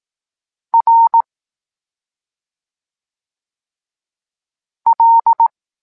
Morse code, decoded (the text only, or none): R L